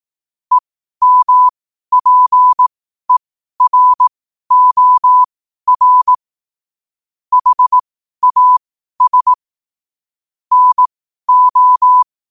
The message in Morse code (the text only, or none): EMPEROR HAS NO